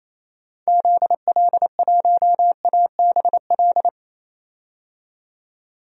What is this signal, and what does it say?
Morse code, heard as a keyed tone, about 28 wpm, 700 Hz. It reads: ZL1ABL
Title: DXCC-100 Call Signs - 6 Characters - 28wpm